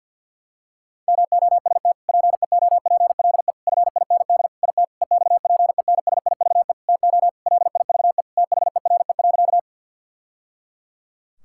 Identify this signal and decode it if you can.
Morse code, heard as a keyed tone, about 50 wpm, 700 Hz. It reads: MOST PEOPLE FIND IT EXPENSIVE TO LIVE THERE.
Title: Sentences from Top 1000 Words - Review Part 6 - 50wpm